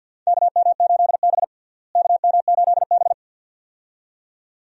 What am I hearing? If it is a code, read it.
Morse code, heard as a keyed tone, about 46 wpm, 700 Hz. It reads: KM8B KM8B
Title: USA Call Signs - 4 Characters - Repeated 2x - 50wpm